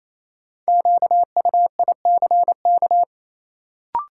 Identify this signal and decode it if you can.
Morse code, heard as a keyed tone, about 28 wpm, 700 Hz. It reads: QUICK
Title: Sentences from Top 500 Words - New Word Drill - 28wpm